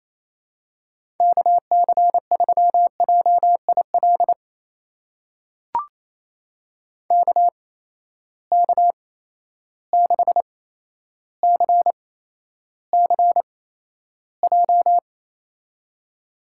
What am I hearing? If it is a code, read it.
Morse code, heard as a keyed tone, about 28 wpm, spaced as for 12 wpm, 700 Hz. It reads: KC3JIL KK6CCJ
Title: USA Call Signs - 6 Characters - 8x Character Spacing - 28wpm